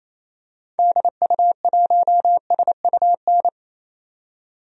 Morse code, heard as a keyed tone, about 28 wpm, 700 Hz. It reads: DU1SUN
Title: DXCC-100 Call Signs - 6 Characters - 28wpm